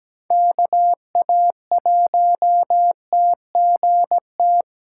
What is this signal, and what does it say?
Morse code, heard as a keyed tone, about 17 wpm, 700 Hz. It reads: KA1TGT
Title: USA Call Signs - 6 Characters - 2x Character Spacing - 17wpm